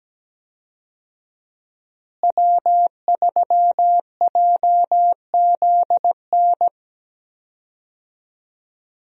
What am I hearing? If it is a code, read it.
Morse code, heard as a keyed tone, about 17 wpm, 700 Hz. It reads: W3JZN
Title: USA Call Signs - 5 Characters - Repeated 2x - 17wpm